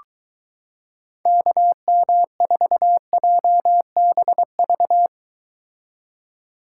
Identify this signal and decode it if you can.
Morse code, heard as a keyed tone, about 23 wpm, 700 Hz. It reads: KM4JBV